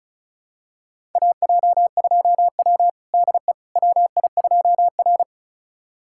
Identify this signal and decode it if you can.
Morse code, heard as a keyed tone, about 36 wpm, 700 Hz. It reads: AJ2W DE WI2R
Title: USA Callsign DE Callsign - 4 Characters - 35wpm